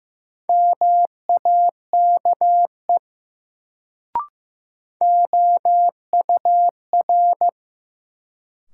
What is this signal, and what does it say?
Morse code, heard as a keyed tone, about 15 wpm, 700 Hz. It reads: MAKE OUR
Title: Sets of 2 Words - New Word Drill - 15wpm